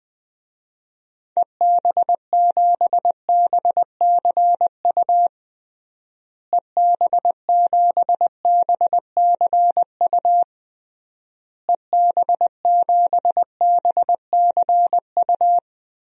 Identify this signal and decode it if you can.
Morse code, heard as a keyed tone, about 20 wpm, 700 Hz. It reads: EB7BCU EB7BCU EB7BCU
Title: DXCC-100 Call Signs - 6 Characters - Repeated 3x - 20wpm